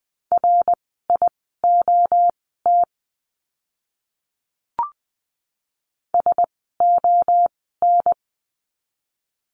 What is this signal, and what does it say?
Morse code, heard as a keyed tone, about 20 wpm, spaced as for 14 wpm, 700 Hz. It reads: RIOT SON
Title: 061 Lesson 11f - Letter h - Words 20wpm